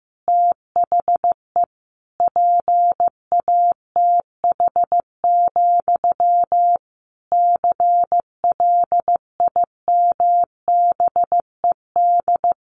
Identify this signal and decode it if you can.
Morse code, heard as a keyed tone, about 15 wpm, 700 Hz. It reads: THE PATH, CLIMBED
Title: A Sound Of Thunder 15wpm